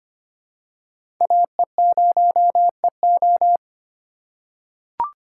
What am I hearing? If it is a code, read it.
Morse code, heard as a keyed tone, about 25 wpm, 700 Hz. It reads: AE0EO
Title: USA Call Signs - 5 Characters - 2x Character Spacing - 25wpm